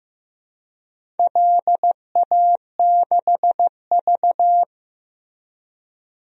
Morse code, heard as a keyed tone, about 15 wpm, 700 Hz. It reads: LA6V
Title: DXCC-100 Call Signs - 4 Characters - Repeated 2x - 15wpm